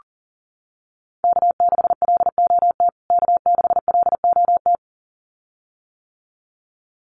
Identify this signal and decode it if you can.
Morse code, heard as a keyed tone, about 40 wpm, 700 Hz. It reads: K6LOT K6LOT
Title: USA Call Signs Repeated 2x 40wpm